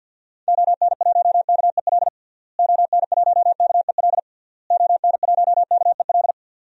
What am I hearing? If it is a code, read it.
Morse code, heard as a keyed tone, about 51 wpm, 700 Hz. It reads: ON1KEL ON1KEL ON1KEL
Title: DXCC-100 Call Signs - 6 Characters - Repeated 3x - 50wpm